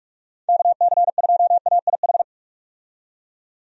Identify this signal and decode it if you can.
Morse code, heard as a keyed tone, about 46 wpm, 700 Hz. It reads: KC2AIH
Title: USA Call Signs - 6 Characters - 45wpm